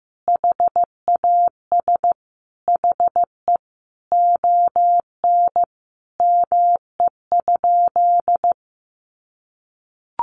Morse code, heard as a keyed tone, about 15 wpm, 700 Hz. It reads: HAS HE ON ME?
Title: Sentences from Top 1500 Words 15wpm